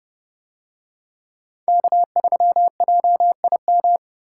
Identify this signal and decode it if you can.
Morse code, heard as a keyed tone, about 28 wpm, 700 Hz. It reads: K3JIM